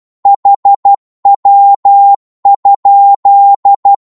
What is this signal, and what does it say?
Morse code, heard as a keyed tone, about 12 wpm, 800 Hz. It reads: HW?
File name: Morse / qso-12wpm